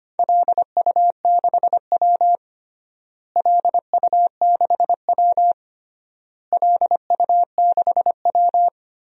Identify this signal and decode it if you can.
Morse code, heard as a keyed tone, about 25 wpm, 700 Hz. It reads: LU6W LU6W LU6W
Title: DXCC-100 Call Signs - 4 Characters - Repeated 3x - 25wpm